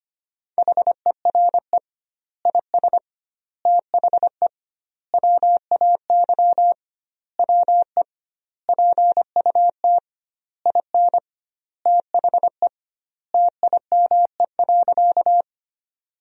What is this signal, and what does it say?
Morse code, heard as a keyed tone, about 25 wpm, 700 Hz. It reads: HERE IS THE WAY WE PUT IN THE TIME.
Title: 2x Word-Spacing - Sentences from Top 200 Words - 25wpm